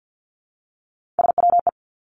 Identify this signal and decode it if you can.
Morse code, heard as a keyed tone, about 50 wpm, 700 Hz. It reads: SWE
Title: Three Challenging Letter-Number Combinations 50wpm